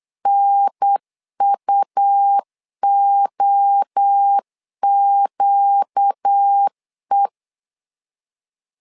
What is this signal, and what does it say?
Morse code, heard as a keyed tone, about 8 wpm, 800 Hz. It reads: NUOQE
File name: 4 Q 8wpm